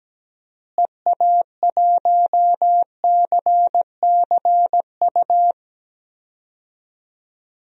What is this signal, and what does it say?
Morse code, heard as a keyed tone, about 17 wpm, 700 Hz. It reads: EA1CCU
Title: DXCC-100 Call Signs - 6 Characters - Repeated 3x - 17wpm